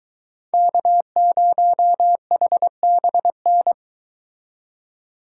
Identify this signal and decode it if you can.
Morse code, heard as a keyed tone, about 23 wpm, 700 Hz. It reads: K0HBN